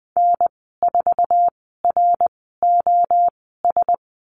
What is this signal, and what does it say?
Morse code, heard as a keyed tone, about 20 wpm, spaced as for 15 wpm, 700 Hz. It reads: N4ROS
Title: 088 Lesson 15g - Number 5 - Call Signs 20wpm